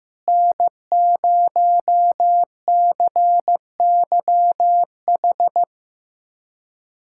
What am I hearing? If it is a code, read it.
Morse code, heard as a keyed tone, about 16 wpm, 700 Hz. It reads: N0CYH
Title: USA Callsign DE Callsign 15wpm